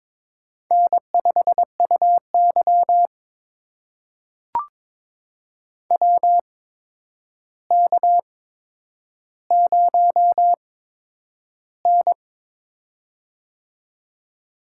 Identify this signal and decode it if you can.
Morse code, heard as a keyed tone, about 22 wpm, spaced as for 9 wpm, 700 Hz. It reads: N5UY WK0N
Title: USA Call Signs - 4 Characters - 8x Character Spacing - 22wpm